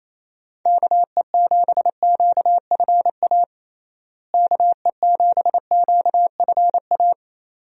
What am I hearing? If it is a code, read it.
Morse code, heard as a keyed tone, about 28 wpm, 700 Hz. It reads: KE7QFA KE7QFA
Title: USA Call Signs - 6 Characters - Repeated 2x - 28wpm